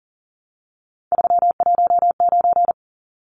Morse code, heard as a keyed tone, about 40 wpm, 700 Hz. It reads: 319
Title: Numbers - 3-Digit 40wpm